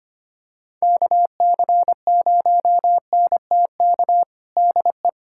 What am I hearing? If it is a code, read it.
Morse code, heard as a keyed tone, about 25 wpm, 700 Hz. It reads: KC0NTK DE
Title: USA Callsign DE Callsign 25wpm